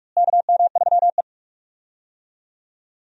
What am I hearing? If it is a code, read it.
Morse code, heard as a keyed tone, about 46 wpm, 700 Hz. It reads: KM3E
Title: USA Call Signs - 4 Characters - Repeated 3x - 45wpm